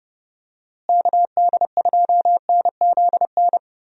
Morse code, heard as a keyed tone, about 30 wpm, 700 Hz. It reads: KD2NZN
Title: USA Call Signs - 6 Characters - Repeated 2x - 30wpm